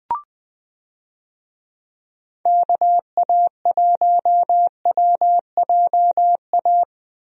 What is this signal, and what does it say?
Morse code, heard as a keyed tone, about 21 wpm, 700 Hz. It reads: KA1WJA
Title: USA Call Signs - 6 Characters - Repeated 3x - 20wpm